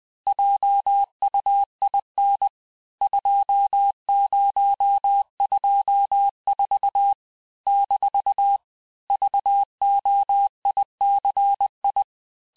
Morse code, heard as a keyed tone, about 20 wpm, 800 Hz. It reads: JUIN 2024 - VOICI
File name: cours-du-lundi files / cours-1584-teste-chiffre-vitesse-19mpm